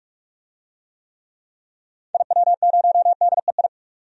Morse code, heard as a keyed tone, about 42 wpm, 700 Hz. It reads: IW0DEI
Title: DXCC-100 Call Signs - 6 Characters - 45wpm